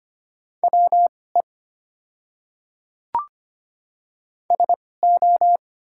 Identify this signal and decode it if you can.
Morse code, heard as a keyed tone, about 25 wpm, spaced as for 18 wpm, 690 Hz. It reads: WE SO
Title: 2x Character Spacing - Top 100 Words - 25wpm